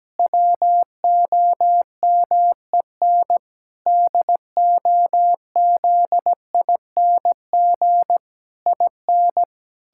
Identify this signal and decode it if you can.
Morse code, heard as a keyed tone, about 17 wpm, 700 Hz. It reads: WOMEN DOZING IN
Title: Pangram Challenge v2 - 17wpm